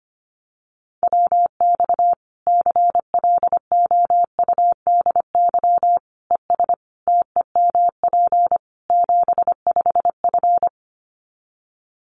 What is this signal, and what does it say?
Morse code, heard as a keyed tone, about 25 wpm, 700 Hz. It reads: WX CLOUDY ES TEMP 75F
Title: Common QSO Phrases 25wpm